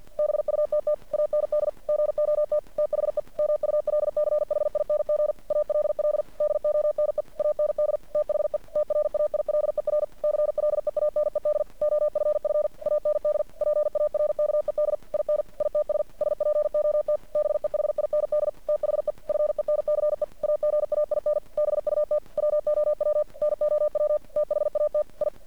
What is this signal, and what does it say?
Morse code, heard as a keyed tone, about 49 wpm, 600 Hz. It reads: BUTT AND GOT THE MULCHING ALL DONE AND THE TRAILER CLEANED OFF AND PARKED IN ITS SPOT BEHIND THE FENCE AGAIN BUT WOW NOW THAT I
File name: archived / CRSnet